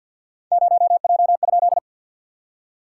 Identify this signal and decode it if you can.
Morse code, heard as a keyed tone, about 51 wpm, 660 Hz. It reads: 0J?
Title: SOTA QSO Practice 50wpm